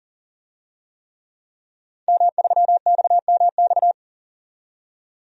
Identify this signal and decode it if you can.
Morse code, heard as a keyed tone, about 40 wpm, 700 Hz. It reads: M3XMX
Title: DXCC-100 Call Signs - 5 Characters - 40wpm